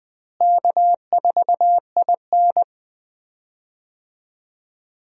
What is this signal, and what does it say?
Morse code, heard as a keyed tone, about 20 wpm, 700 Hz. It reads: K4IN
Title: USA Call Signs - 4 Characters - Repeated 3x - 20wpm